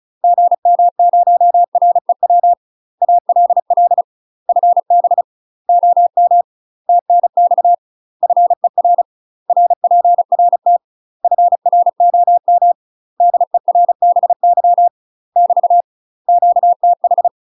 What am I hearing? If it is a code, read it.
Morse code, heard as a keyed tone, about 34 wpm, 700 Hz. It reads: GM0REW ALL FB OM TNX FER RPRT FROM DERBY <BT> QTH
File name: UK MASTER QSOs TRAINING / 35WPM / UK MASTER QSOs 0029